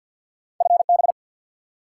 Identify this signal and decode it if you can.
Morse code, heard as a keyed tone, about 53 wpm, 700 Hz. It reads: FB
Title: Top 180 QSO Elements 50wpm